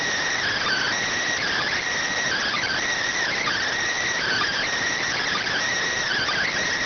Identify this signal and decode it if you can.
Emergency Alert System (SAME) tones (header bursts), heard as harsh buzzing bursts. It is heard from 0.9 to 6.4 s.